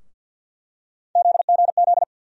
Morse code, heard as a keyed tone, about 48 wpm, 700 Hz. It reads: ZGZ
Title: USA Call Sign Postfixes - 3 Letters 50wpm